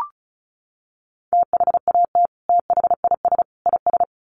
Morse code, heard as a keyed tone, about 34 wpm, 700 Hz. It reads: THAT THIS IS
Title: Sets of 3 Words 35wpm